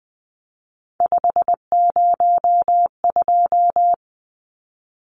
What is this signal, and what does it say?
Morse code, heard as a keyed tone, about 20 wpm, 700 Hz. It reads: 502
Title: Numbers - 3-Digit 20wpm